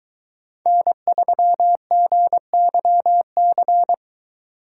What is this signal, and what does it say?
Morse code, heard as a keyed tone, about 23 wpm, 700 Hz. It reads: N3GYC